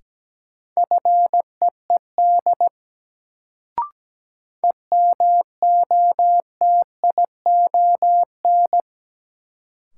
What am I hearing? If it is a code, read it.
Morse code, heard as a keyed tone, about 17 wpm, 700 Hz. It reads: FEED EMOTION
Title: Sentences from Top 1600 Words - New Word Drill - 17wpm